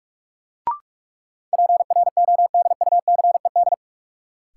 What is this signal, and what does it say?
Morse code, heard as a keyed tone, about 46 wpm, 700 Hz. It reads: PRODUCED
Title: Sentences from Top 1000 Words - Review Part 6 - New Word Drill - 45wpm